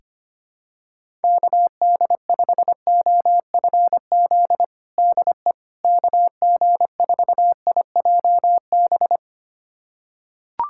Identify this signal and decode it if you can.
Morse code, heard as a keyed tone, about 25 wpm, 700 Hz. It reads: KD5OFZ DE KG4IJB
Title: USA Callsign DE Callsign - 6 Characters - 25wpm